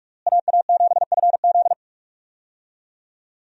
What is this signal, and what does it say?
Morse code, heard as a keyed tone, about 42 wpm, 700 Hz. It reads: AA7FZ
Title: USA Call Signs - 5 Characters - Repeated 2x - 45wpm